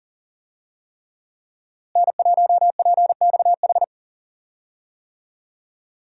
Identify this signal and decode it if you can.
Morse code, heard as a keyed tone, about 40 wpm, 700 Hz. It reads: N1PXH
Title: USA Call Signs - 5 Characters - Repeated 3x - 40wpm